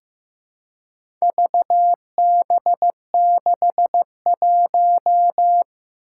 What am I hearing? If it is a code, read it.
Morse code, heard as a keyed tone, about 15 wpm, 700 Hz. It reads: VB61
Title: Four Challenging Letter-Number Combinations 15wpm